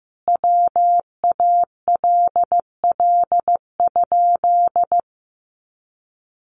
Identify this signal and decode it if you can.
Morse code, heard as a keyed tone, about 15 wpm, 700 Hz. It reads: WALL?
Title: Sentences from Top 1100 Words 15wpm